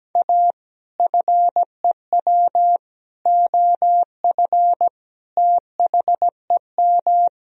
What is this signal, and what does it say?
Morse code, heard as a keyed tone, about 17 wpm, 700 Hz. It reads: A FEW OF THEM
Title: Sentences from Top 1000 Words - Review Part 1 - 17wpm